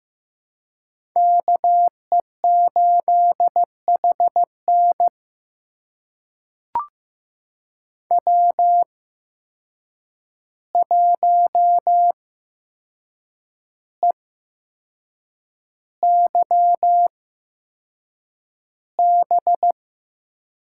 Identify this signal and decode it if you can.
Morse code, heard as a keyed tone, about 15 wpm, spaced as for 6 wpm, 700 Hz. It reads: KE8HN W1EYB
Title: USA Call Signs - 5 Characters - 8x Character Spacing - 15wpm